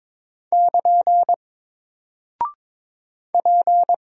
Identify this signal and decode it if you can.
Morse code, heard as a keyed tone, about 21 wpm, 700 Hz. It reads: <KN> P
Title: Single Character (Letter:Number:Punctuation:Prosign) Alt 22wpm